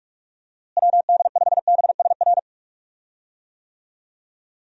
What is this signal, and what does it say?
Morse code, heard as a keyed tone, about 46 wpm, 700 Hz. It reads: WD5BSR
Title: USA Call Signs - 6 Characters - Repeated 3x - 45wpm